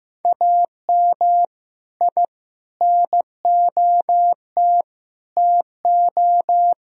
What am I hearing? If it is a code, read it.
Morse code, heard as a keyed tone, about 15 wpm, 700 Hz. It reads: AM I NOT TO
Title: Sentences from Top 1000 Words - Review Part 2 - 15wpm